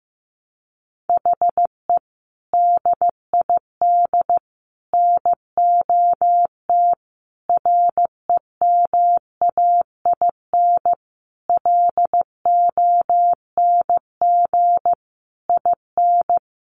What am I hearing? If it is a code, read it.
Morse code, heard as a keyed tone, about 15 wpm, 700 Hz. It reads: HE DID NOT REMAIN LONG IN
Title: Sentences from Top 1500 Words 15wpm